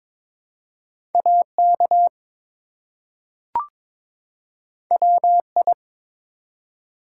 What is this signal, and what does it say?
Morse code, heard as a keyed tone, about 22 wpm, 700 Hz. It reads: AK WI
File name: ARRL Sections v2 - 22wpm